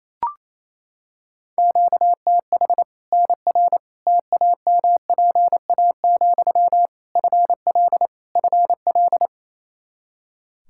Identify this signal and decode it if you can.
Morse code, heard as a keyed tone, about 27 wpm, 700 Hz. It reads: QTH NR TAMPA, FL FL
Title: K1USN SST QSO Practice 28wpm